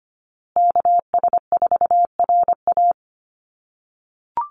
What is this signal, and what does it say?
Morse code, heard as a keyed tone, about 25 wpm, 700 Hz. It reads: KS4RA
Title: USA Call Signs Repeated 2x 25wpm